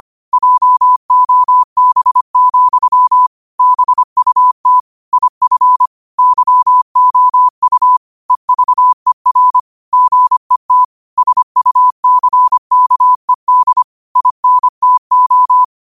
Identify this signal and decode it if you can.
Morse code, heard as a keyed tone, about 25 wpm, 1000 Hz. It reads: JOB, BUT IF YOU EVER GET SUCKED INTO
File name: qotdfiles / 25wpm